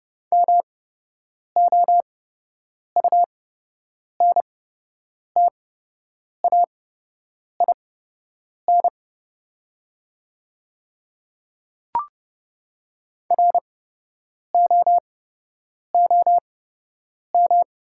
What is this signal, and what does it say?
Morse code, heard as a keyed tone, about 30 wpm, spaced as for 8 wpm, 700 Hz. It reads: MOUNTAIN ROOM
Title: 8x Character Spacing - Top 300 Words - 30wpm